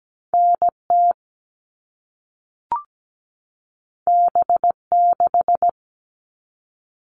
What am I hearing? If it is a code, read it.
Morse code, heard as a keyed tone, about 17 wpm, 700 Hz. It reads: NT B6
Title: All Cut Numbers - 2-Digit 17wpm